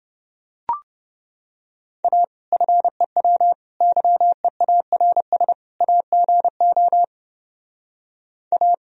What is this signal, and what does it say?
Morse code, heard as a keyed tone, about 30 wpm, 700 Hz. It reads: A FEW YEARS AGO A